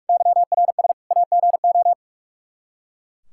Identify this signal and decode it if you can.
Morse code, heard as a keyed tone, about 42 wpm, 700 Hz. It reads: YRS AGO
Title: Practice QSO - Exchange 3 - More Info - 45wpm